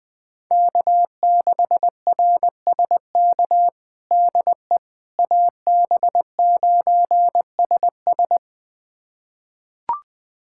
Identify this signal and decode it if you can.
Morse code, heard as a keyed tone, about 20 wpm, 700 Hz. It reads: K6RSK DE AB9SS
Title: USA Callsign DE Callsign - 5 Characters - 20wpm